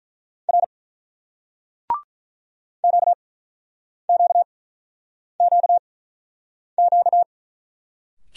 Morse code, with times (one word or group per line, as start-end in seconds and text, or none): 0.5-0.6 R
2.8-7.2 QQQQ